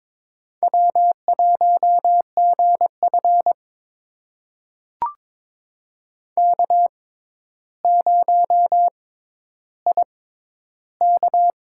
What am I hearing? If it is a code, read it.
Morse code, heard as a keyed tone, about 22 wpm, spaced as for 11 wpm, 700 Hz. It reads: W1GF K0IK
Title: USA Call Signs - 4 Characters - 6x Character Spacing - 22wpm